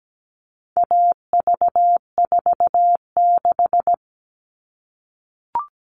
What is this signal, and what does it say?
Morse code, heard as a keyed tone, about 17 wpm, 700 Hz. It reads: AV46
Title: All Cut Numbers - 4-Digit 17wpm